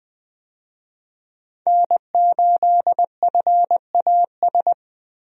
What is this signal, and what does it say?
Morse code, heard as a keyed tone, about 20 wpm, 700 Hz. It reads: N8FAS
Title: USA Call Signs - 5 Characters - Repeated 2x - 20wpm